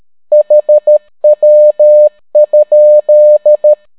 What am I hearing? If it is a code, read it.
Morse code, heard as a keyed tone, about 13 wpm, 600 Hz. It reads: HW?